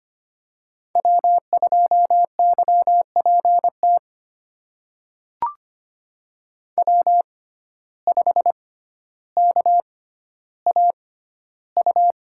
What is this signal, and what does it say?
Morse code, heard as a keyed tone, about 25 wpm, spaced as for 13 wpm, 700 Hz. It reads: W2YPT W5KAU
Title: USA Call Signs - 5 Characters - 6x Character Spacing - 25wpm